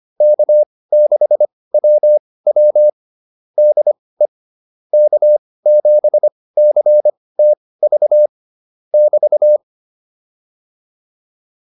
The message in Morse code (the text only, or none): K6WW DE K7CTV <BT>